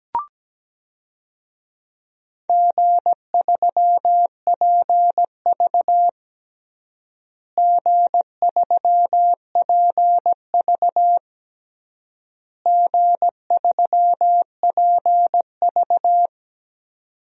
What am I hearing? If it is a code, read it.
Morse code, heard as a keyed tone, about 17 wpm, 700 Hz. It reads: G3PV G3PV G3PV
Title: DXCC-100 Call Signs - 4 Characters - Repeated 3x - 17wpm